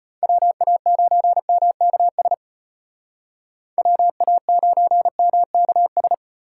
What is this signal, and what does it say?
Morse code, heard as a keyed tone, about 33 wpm, 700 Hz. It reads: WA9MKS WA9MKS